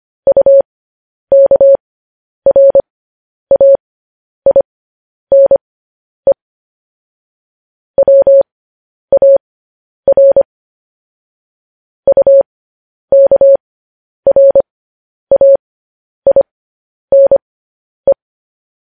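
Morse code, heard as a keyed tone, about 25 wpm, spaced as for 10 wpm, 550 Hz. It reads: UKRAINE WAR UKRAINE
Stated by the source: Morse code news headlines sent as a reading transcription, preceded by vvv.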